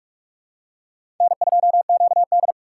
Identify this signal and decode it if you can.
Morse code, heard as a keyed tone, about 46 wpm, 700 Hz. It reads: N2QD
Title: USA Call Signs - 4 Characters - 6x Character Spacing - 45wpm